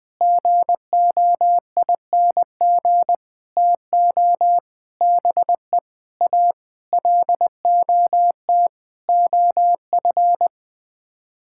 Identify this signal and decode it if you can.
Morse code, heard as a keyed tone, about 20 wpm, 700 Hz. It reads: GOING TO BE A LOT OF
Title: Sets of 8 Words - 20wpm